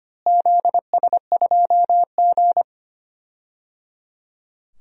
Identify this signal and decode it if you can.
Morse code, heard as a keyed tone, about 25 wpm, 700 Hz. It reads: ZS2G
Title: DXCC-100 Call Signs - 4 Characters - Repeated 3x - 25wpm